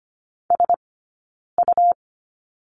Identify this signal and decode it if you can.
Morse code, heard as a keyed tone, about 25 wpm, 700 Hz. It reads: S U
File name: Lesson 17d - Letter u - Two Characters 25wpm